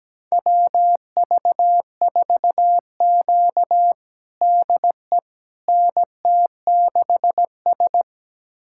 Morse code, heard as a keyed tone, about 17 wpm, 700 Hz. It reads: WV4Q DE NT6S
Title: USA Callsign DE Callsign - 4 Characters - 17wpm